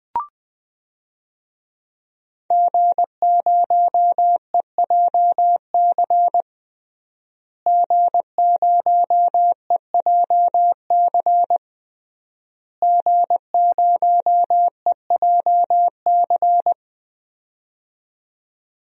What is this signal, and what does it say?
Morse code, heard as a keyed tone, about 21 wpm, 700 Hz. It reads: G0EJC G0EJC G0EJC
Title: DXCC-100 Call Signs - 5 Characters - Repeated 3x - 20wpm